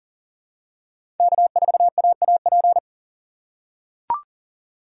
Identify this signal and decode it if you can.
Morse code, heard as a keyed tone, about 40 wpm, 700 Hz. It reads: K4AAP
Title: USA Call Signs - 5 Characters - 4x Character Spacing - 40wpm